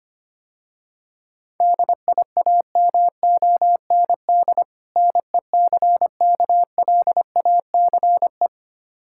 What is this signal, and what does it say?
Morse code, heard as a keyed tone, about 25 wpm, 700 Hz. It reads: DIAMOND NECKLACE
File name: practice / 25wpm / Difficult Words - Speed-Racing with Hints - 25wpm